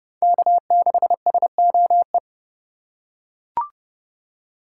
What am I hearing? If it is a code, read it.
Morse code, heard as a keyed tone, about 29 wpm, 700 Hz. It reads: K6SOE